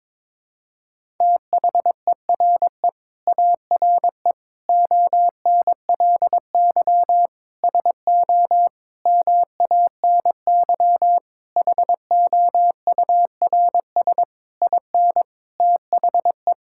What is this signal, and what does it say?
Morse code, heard as a keyed tone, about 22 wpm, 700 Hz. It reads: THERE ARE ONLY SO MANY HOURS IN THE
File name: Sets of 9 Words - 22wpm